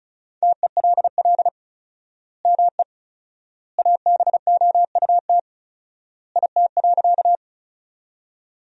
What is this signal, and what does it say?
Morse code, heard as a keyed tone, about 36 wpm, 700 Hz. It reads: TELL ME ABOUT IT.
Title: 4x Word-Spacing - Sentences from Top 200 Words - 35wpm